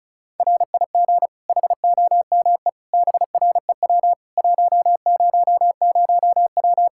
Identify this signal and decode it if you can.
Morse code, heard as a keyed tone, about 36 wpm, 700 Hz. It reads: RIG HOME BREW 100W
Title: K1USN SST QSO Practice 35wpm